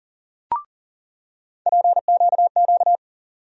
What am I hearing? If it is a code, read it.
Morse code, heard as a keyed tone, about 40 wpm, 700 Hz. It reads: PQQ
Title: USA Call Sign Postfixes - 3 Letters 40wpm